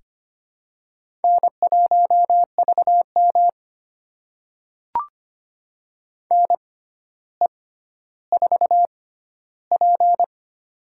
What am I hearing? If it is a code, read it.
Morse code, heard as a keyed tone, about 25 wpm, spaced as for 13 wpm, 700 Hz. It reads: N1VM NE4P
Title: USA Call Signs - 4 Characters - 6x Character Spacing - 25wpm